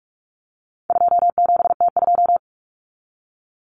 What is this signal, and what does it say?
Morse code, heard as a keyed tone, about 44 wpm, 700 Hz. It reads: 27T2
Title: Common Cut Numbers - 4-Digit 45wpm